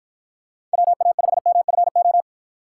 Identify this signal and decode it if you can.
Morse code, heard as a keyed tone, about 49 wpm, 700 Hz. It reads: WA5MFO